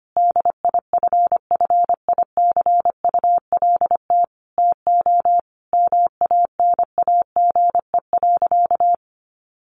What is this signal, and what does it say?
Morse code, heard as a keyed tone, about 25 wpm, 700 Hz. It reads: DIFFICULT TO MANAGE.
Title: Sentences from Top 1400 Words 25wpm